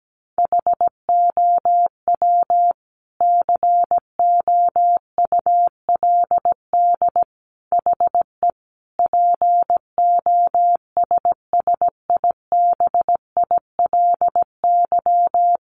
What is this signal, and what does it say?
Morse code, heard as a keyed tone, about 17 wpm, 700 Hz. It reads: HOW COULD HE POSSIBILY
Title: Sentences from Top 1300 Words 17wpm